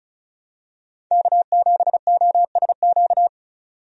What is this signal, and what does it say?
Morse code, heard as a keyed tone, about 34 wpm, 700 Hz. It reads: K7OSQ
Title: USA Call Signs - 5 Characters - 35wpm